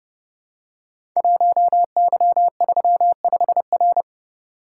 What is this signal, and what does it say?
Morse code, heard as a keyed tone, about 30 wpm, 700 Hz. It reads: 1Y35R
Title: Code Groups - 5 Letter:Numbers - 30wpm